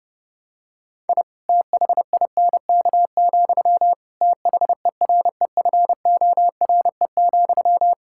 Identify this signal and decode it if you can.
Morse code, heard as a keyed tone, about 32 wpm, 700 Hz. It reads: I THINK, THEREFORE,
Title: Sentences from Top 1000 Words - Review Part 5 - 30wpm